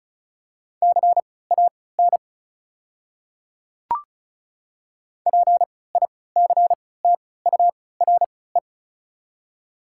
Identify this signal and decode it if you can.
Morse code, heard as a keyed tone, about 34 wpm, spaced as for 20 wpm, 690 Hz. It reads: CAN PICTURE
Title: 3x Character Spacing - Top 300 Words - 35wpm